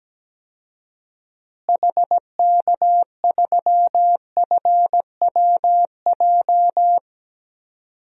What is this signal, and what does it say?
Morse code, heard as a keyed tone, about 17 wpm, 700 Hz. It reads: HK3FWJ
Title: DXCC-100 Call Signs - 6 Characters - Repeated 2x - 17wpm